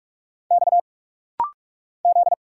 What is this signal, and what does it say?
Morse code, heard as a keyed tone, about 45 wpm, 700 Hz. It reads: X Z
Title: Single Letter-Number - Rapid-Fire 45wpm